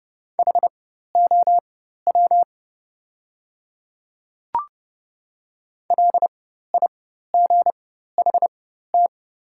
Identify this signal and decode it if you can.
Morse code, heard as a keyed tone, about 30 wpm, spaced as for 14 wpm, 700 Hz. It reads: HOW LIGHT
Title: 4x Character Spacing - Top 300 Words - 30wpm